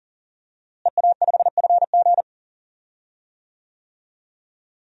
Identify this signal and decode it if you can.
Morse code, heard as a keyed tone, about 40 wpm, 700 Hz. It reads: EA5FG
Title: DXCC-100 Call Signs - 5 Characters - Repeated 2x - 40wpm